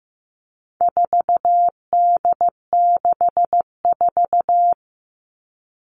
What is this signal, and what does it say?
Morse code, heard as a keyed tone, about 15 wpm, 700 Hz. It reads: 4D64
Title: All Cut Numbers - 4-Digit 15wpm